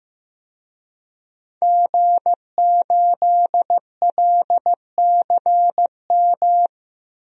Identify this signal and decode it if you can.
Morse code, heard as a keyed tone, about 15 wpm, 700 Hz. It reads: G8LCM
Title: DXCC-100 Call Signs - 5 Characters - 15wpm